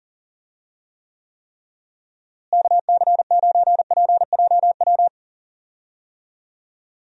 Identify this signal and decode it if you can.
Morse code, heard as a keyed tone, about 40 wpm, 700 Hz. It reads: KC9PJW
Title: USA Call Signs - 6 Characters - Repeated 3x - 40wpm